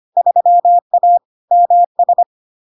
Morse code, heard as a keyed tone, about 25 wpm, 700 Hz. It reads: 3A MS
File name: LARK-FD-2022-Log-in-CW-25wpm-700-Hz